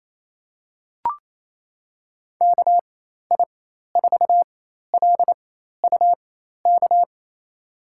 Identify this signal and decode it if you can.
Morse code, heard as a keyed tone, about 28 wpm, spaced as for 13 wpm, 700 Hz. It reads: KI4LUK
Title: USA Call Signs - 6 Characters - 4x Character Spacing - 28wpm